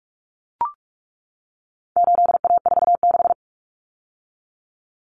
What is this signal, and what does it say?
Morse code, heard as a keyed tone, about 44 wpm, 700 Hz. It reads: 8A46
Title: Common Cut Numbers - 4-Digit 45wpm